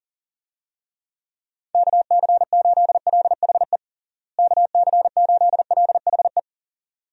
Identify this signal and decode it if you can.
Morse code, heard as a keyed tone, about 40 wpm, 700 Hz. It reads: KC8LHE KC8LHE
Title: USA Call Signs - 6 Characters - Repeated 2x - 40wpm